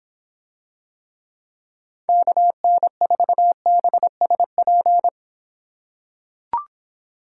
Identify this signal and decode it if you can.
Morse code, heard as a keyed tone, about 26 wpm, 700 Hz. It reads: KN4BSP